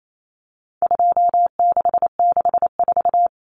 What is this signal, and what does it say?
Morse code, heard as a keyed tone, about 28 wpm, 700 Hz. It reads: 2664
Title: Common Cut Numbers - 4-Digit 28wpm